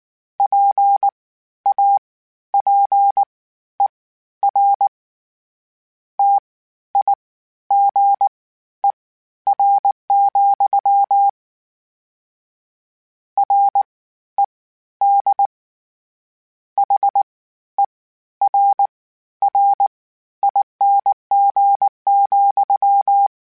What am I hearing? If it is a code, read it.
Morse code, heard as a keyed tone, about 19 wpm, spaced as for 12 wpm, 800 Hz. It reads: PAPER TIGER, RED HERRING,
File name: Misc / Misc-mp3s / WordPairs sp @18